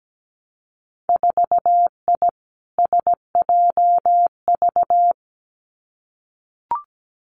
Morse code, heard as a keyed tone, about 17 wpm, 700 Hz. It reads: 4I SJV
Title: Winter Field Day Exchanges Encore 17wpm